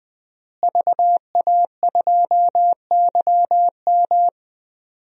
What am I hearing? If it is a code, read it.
Morse code, heard as a keyed tone, about 21 wpm, 700 Hz. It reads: VA2YM
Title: DXCC-100 Call Signs - 5 Characters - Repeated 3x - 20wpm